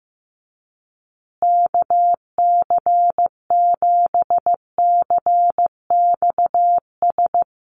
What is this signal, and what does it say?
Morse code, heard as a keyed tone, about 15 wpm, 700 Hz. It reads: KC7CXS
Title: USA Call Signs Repeated 3x 15wpm